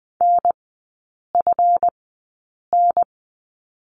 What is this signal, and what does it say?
Morse code, heard as a keyed tone, about 20 wpm, spaced as for 8 wpm, 700 Hz. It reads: NFN
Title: 138 Lesson 23e - Letter f - Three Characters 20wpm